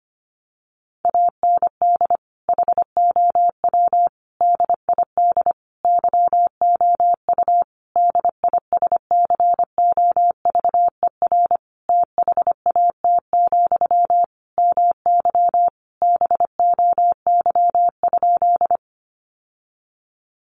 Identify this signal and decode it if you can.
Morse code, heard as a keyed tone, about 25 wpm, 700 Hz. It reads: AND HOW DID YOU DISCOVER THAT, MY BOY?
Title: Sentences from Top 1100 Words 25wpm